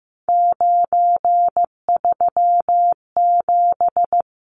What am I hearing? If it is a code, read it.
Morse code, heard as a keyed tone, about 15 wpm, 700 Hz. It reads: 937
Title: Numbers - 3-Digit 15wpm